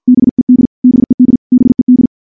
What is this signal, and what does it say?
Morse code, heard as a keyed tone, about 50 wpm, 250 Hz. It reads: BED BED BED
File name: morse / morse-training-50WPM